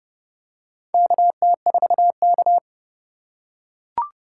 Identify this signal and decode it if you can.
Morse code, heard as a keyed tone, about 31 wpm, 700 Hz. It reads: KT4K
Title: USA Call Signs - 4x Character Spacing - 30wpm